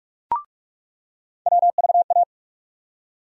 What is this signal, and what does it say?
Morse code, heard as a keyed tone, about 46 wpm, 700 Hz. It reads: WVA
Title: USA Call Sign Postfixes - 3 Letters 45wpm